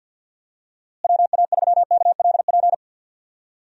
Morse code, heard as a keyed tone, about 47 wpm, 700 Hz. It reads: WA3KLP
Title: USA Call Signs - 6 Characters - Repeated 2x - 50wpm